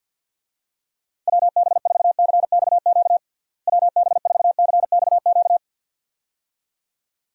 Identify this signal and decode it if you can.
Morse code, heard as a keyed tone, about 53 wpm, 700 Hz. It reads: WB4CXQ WB4CXQ
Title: USA Call Signs - 6 Characters - Repeated 2x - 50wpm